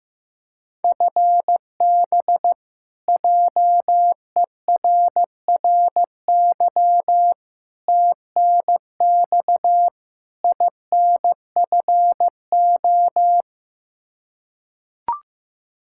Morse code, heard as a keyed tone, about 15 wpm, 700 Hz. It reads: FB JERRY TNX INFO
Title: K1USN SST QSO Practice 15wpm